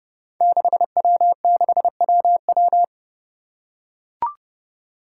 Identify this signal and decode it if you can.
Morse code, heard as a keyed tone, about 30 wpm, 700 Hz. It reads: 6W6WW
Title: Five Challenging Letter-Number Combinations 30wpm